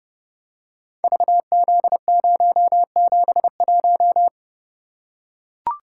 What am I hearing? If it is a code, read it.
Morse code, heard as a keyed tone, about 30 wpm, 700 Hz. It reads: VZ071
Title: Code Groups - 5 Letter:Numbers - 30wpm